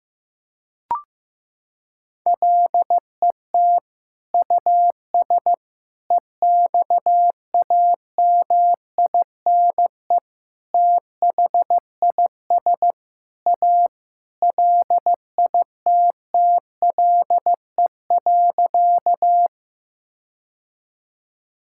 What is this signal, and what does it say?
Morse code, heard as a keyed tone, about 15 wpm, 700 Hz. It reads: LET US EXAMINE THIS A LITTLE.
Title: Sentences from Top 1600 Words 15wpm